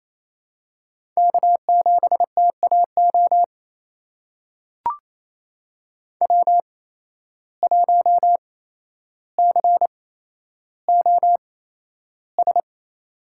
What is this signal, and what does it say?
Morse code, heard as a keyed tone, about 28 wpm, spaced as for 12 wpm, 700 Hz. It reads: K7TAO W1COS
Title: USA Call Signs - 5 Characters - 8x Character Spacing - 28wpm